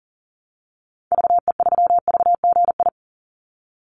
Morse code, heard as a keyed tone, about 40 wpm, 700 Hz. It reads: VE3VGI
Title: Call Signs Repeated 2x - 40wpm